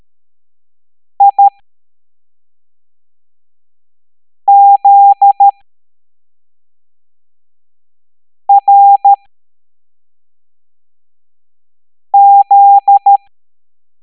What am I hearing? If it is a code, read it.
Morse code, heard as a keyed tone, about 13 wpm, spaced as for 3 wpm, 800 Hz. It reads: IZRZ